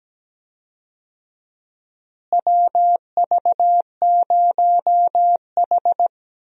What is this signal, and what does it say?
Morse code, heard as a keyed tone, about 17 wpm, 700 Hz. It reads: WV0H
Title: USA Call Signs - 4 Characters - Repeated 3x - 17wpm